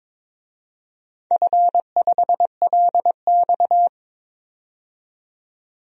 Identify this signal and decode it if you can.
Morse code, heard as a keyed tone, about 22 wpm, 700 Hz. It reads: F5LX
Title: DXCC-100 Call Signs - 4 Characters - 22wpm